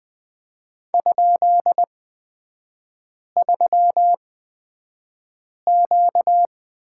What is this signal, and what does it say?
Morse code, heard as a keyed tone, about 20 wpm, spaced as for 5 wpm, 690 Hz. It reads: ?3Q
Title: 236 Lesson 38e - Letter q - Three Characters 20wpm